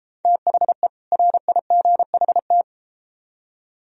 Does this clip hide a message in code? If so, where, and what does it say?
Yes: Morse code from 0.2 to 2.6 s, heard as a keyed tone, about 33 wpm, 700 Hz. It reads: THE RIGHT